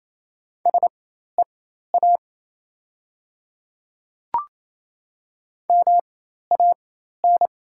Morse code, heard as a keyed tone, about 28 wpm, spaced as for 13 wpm, 700 Hz. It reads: SEA MAN
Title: 4x Character Spacing - Top 500 Words - 28wpm